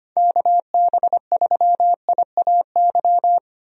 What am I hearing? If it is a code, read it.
Morse code, heard as a keyed tone, about 25 wpm, 700 Hz. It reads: KB3IAY
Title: USA Call Signs - 6 Characters - 25wpm